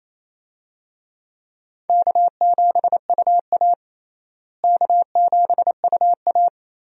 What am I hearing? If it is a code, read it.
Morse code, heard as a keyed tone, about 27 wpm, 700 Hz. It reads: K7UA K7UA
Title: USA Call Signs - 4 Characters - Repeated 3x - 28wpm